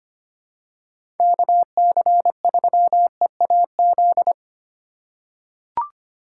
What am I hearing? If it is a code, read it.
Morse code, heard as a keyed tone, about 25 wpm, 700 Hz. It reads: KC3EAZ
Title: USA Call Signs - 6 Characters - 2x Character Spacing - 25wpm